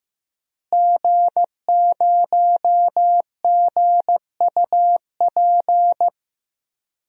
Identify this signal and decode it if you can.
Morse code, heard as a keyed tone, about 15 wpm, 700 Hz. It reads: G0GUP
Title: DXCC-100 Call Signs - 5 Characters - Repeated 3x - 15wpm